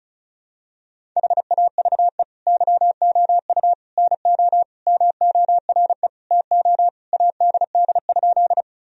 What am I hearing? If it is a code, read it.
Morse code, heard as a keyed tone, about 36 wpm, 690 Hz. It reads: HAVE YOU NO MORE TO ADD?
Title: Sentences from Top 1000 Words - Review Part 1 - 35wpm